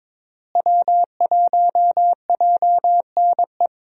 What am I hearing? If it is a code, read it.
Morse code, heard as a keyed tone, about 22 wpm, 700 Hz. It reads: W1JNE
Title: USA Call Signs - 5 Characters - Repeated 3x - 22wpm